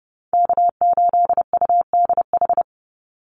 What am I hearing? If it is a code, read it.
Morse code, heard as a keyed tone, about 30 wpm, 700 Hz. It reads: K8UDH
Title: Call Signs - 30wpm